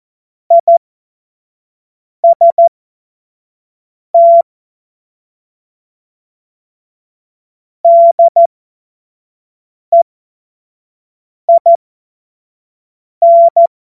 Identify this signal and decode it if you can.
Morse code, heard as a keyed tone, about 14 wpm, spaced as for 5 wpm, 680 Hz. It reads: IST DEIN
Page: cw-witze